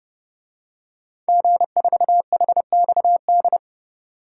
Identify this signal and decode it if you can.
Morse code, heard as a keyed tone, about 30 wpm, 700 Hz. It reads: G4HXD
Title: DXCC-100 Call Signs - 5 Characters - Repeated 2x - 30wpm